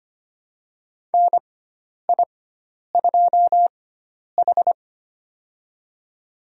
Morse code, heard as a keyed tone, about 25 wpm, spaced as for 10 wpm, 700 Hz. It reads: NI2H
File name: Lesson 16g - Letter c - Call Signs 25wpm